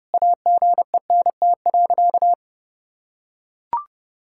Morse code, heard as a keyed tone, about 30 wpm, 700 Hz. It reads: AGENT.
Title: Sentences from Top 1600 Words 30wpm